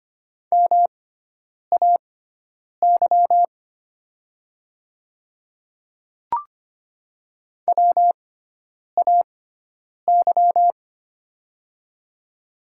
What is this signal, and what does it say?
Morse code, heard as a keyed tone, about 25 wpm, spaced as for 9 wpm, 700 Hz. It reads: MAY WAY
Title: 6x Character Spacing - Top 100 Words - 25wpm